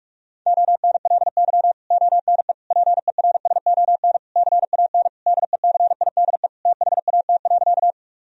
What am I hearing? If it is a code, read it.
Morse code, heard as a keyed tone, about 41 wpm, 700 Hz. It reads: ONLY ONE PERSON CAN DECIDE THAT.
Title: Sentences from Top 1000 Words - Review Part 3 - 45wpm